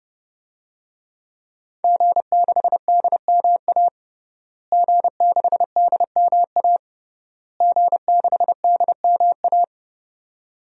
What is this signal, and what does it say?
Morse code, heard as a keyed tone, about 30 wpm, 700 Hz. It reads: G6DMA G6DMA G6DMA
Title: DXCC-100 Call Signs - 5 Characters - Repeated 3x - 30wpm